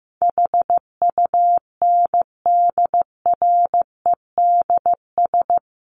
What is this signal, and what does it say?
Morse code, heard as a keyed tone, about 14 wpm, 700 Hz. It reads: HUNDREDS
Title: Sets of 5 Words Encore 15wpm